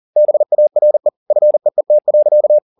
Morse code, heard as a keyed tone, about 40 wpm, 600 Hz. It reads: BARE FEET.